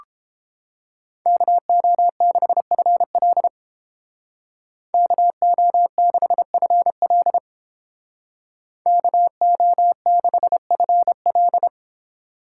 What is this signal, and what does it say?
Morse code, heard as a keyed tone, about 30 wpm, 700 Hz. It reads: KO6FL KO6FL KO6FL